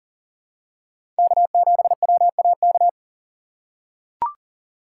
Morse code, heard as a keyed tone, about 40 wpm, 700 Hz. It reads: K7WAK
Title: USA Call Signs - 5 Characters - 4x Character Spacing - 40wpm